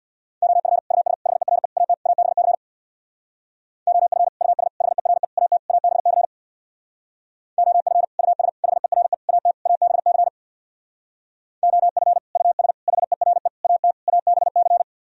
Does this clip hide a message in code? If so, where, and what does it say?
Yes: Morse code from 0.4 to 14.8 s, heard as a keyed tone, about 65 wpm, 700 Hz. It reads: OF US HERE AT ABC OF US HERE AT ABC OF US HERE AT ABC OF US HERE AT ABC